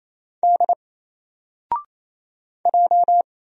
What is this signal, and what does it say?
Morse code, heard as a keyed tone, about 28 wpm, 700 Hz. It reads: D J
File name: Single-Letter-Number-28wpm